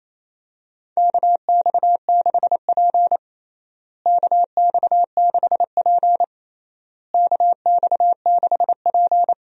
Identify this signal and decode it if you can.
Morse code, heard as a keyed tone, about 28 wpm, 700 Hz. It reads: KX6P KX6P KX6P
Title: USA Call Signs - 4 Characters - Repeated 3x - 28wpm